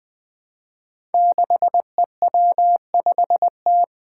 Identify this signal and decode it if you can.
Morse code, heard as a keyed tone, about 20 wpm, 700 Hz. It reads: 6EW5T
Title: Five Challenging Letter-Number Combinations 20wpm